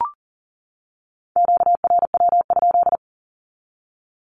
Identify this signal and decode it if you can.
Morse code, heard as a keyed tone, about 37 wpm, 700 Hz. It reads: QRW?
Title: Q-Codes 40wpm